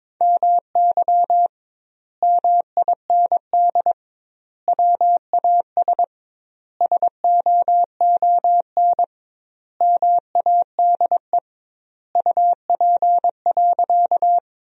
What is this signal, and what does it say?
Morse code, heard as a keyed tone, about 22 wpm, 700 Hz. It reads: MY MIND WAS SOON MADE UP.
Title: 2x Word-Spacing - Sentences from Top 500 Words - 22wpm